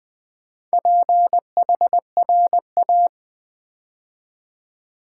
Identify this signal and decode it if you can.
Morse code, heard as a keyed tone, about 20 wpm, 700 Hz. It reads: PHRA
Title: Build A Word - 2 Letters At A Time - 6-Letter Words - 20wpm